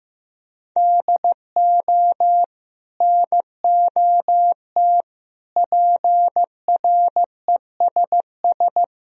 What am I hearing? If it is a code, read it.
Morse code, heard as a keyed tone, about 15 wpm, 700 Hz. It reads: DO NOT PRESS
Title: Sentences from Top 1000 Words - Review Part 1 - 15wpm